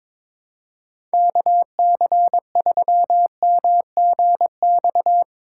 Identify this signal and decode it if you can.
Morse code, heard as a keyed tone, about 22 wpm, 700 Hz. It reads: KC3MGX
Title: USA Call Signs - 6 Characters - 6x Character Spacing - 22wpm